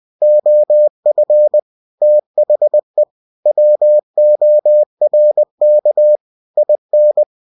Morse code, heard as a keyed tone, about 20 wpm, 600 Hz. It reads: OF THE WORK IN
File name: cw-books / winnie-the-pooh / winnie-20wpm-page-0116